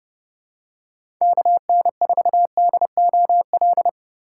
Morse code, heard as a keyed tone, about 30 wpm, 700 Hz. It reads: KN4DOL
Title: USA Call Signs - 2x Character Spacing - 30wpm